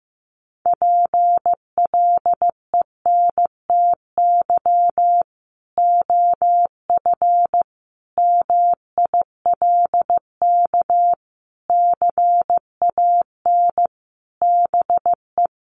Morse code, heard as a keyed tone, about 15 wpm, 700 Hz. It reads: PLENTY OF MILK CAN BE
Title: Sentences from Top 1500 Words 15wpm